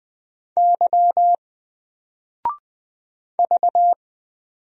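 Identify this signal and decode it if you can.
Morse code, heard as a keyed tone, about 20 wpm, 700 Hz. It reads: Y V
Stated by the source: Single letter and number Morse drill at 20 wpm, from N4LCD.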